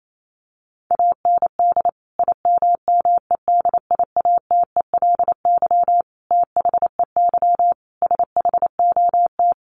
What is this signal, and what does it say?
Morse code, heard as a keyed tone, about 28 wpm, 700 Hz. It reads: AND IMMEDIATELY THEY SHOT
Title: Sentences from Top 1100 Words 28wpm